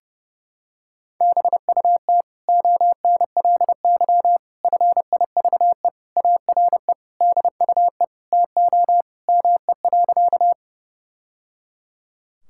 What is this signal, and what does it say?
Morse code, heard as a keyed tone, about 30 wpm, 700 Hz. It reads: BUT ONLY FIVE ARE DUE TO ME.
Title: Sentences from Top 1000 Words - Review Part 6 - 30wpm